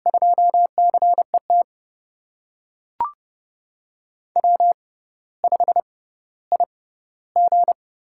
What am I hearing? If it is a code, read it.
Morse code, heard as a keyed tone, about 30 wpm, spaced as for 15 wpm, 700 Hz. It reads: 2CET W5IG
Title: USA Call Signs - 5 Characters - 6x Character Spacing - 30wpm